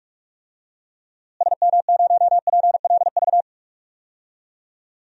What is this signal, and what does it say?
Morse code, heard as a keyed tone, about 46 wpm, 700 Hz. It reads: SM0PLV
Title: DXCC-100 Call Signs - 6 Characters - Repeated 2x - 45wpm